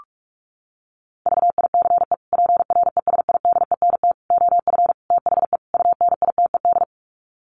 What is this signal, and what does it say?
Morse code, heard as a keyed tone, about 44 wpm, 700 Hz. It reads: VICE PRESIDENT OF THE UNITED
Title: Sets of 5 Words Encore2 45wpm